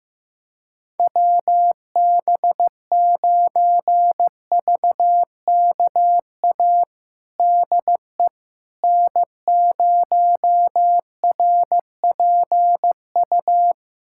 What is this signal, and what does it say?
Morse code, heard as a keyed tone, about 15 wpm, 700 Hz. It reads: WB9VKA DE N0RPU
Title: USA Callsign DE Callsign 15wpm